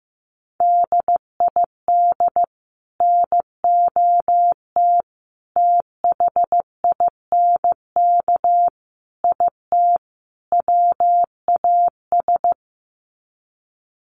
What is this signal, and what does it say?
Morse code, heard as a keyed tone, about 15 wpm, 700 Hz. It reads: DID NOT THINK IT WAS
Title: Sets of 5 Words Encore 15wpm